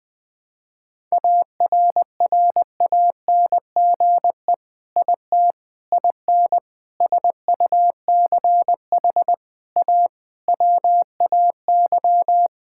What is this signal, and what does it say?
Morse code, heard as a keyed tone, about 20 wpm, 700 Hz. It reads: ARRANGE IT IN SUCH A WAY
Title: Sentences with 6-Word Phrases - Spoken Beginning - 20wpm